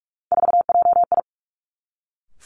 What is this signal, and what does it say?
Morse code, heard as a keyed tone, about 46 wpm, 700 Hz. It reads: 4JI
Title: Three Challenging Letter-Number Combinations 45wpm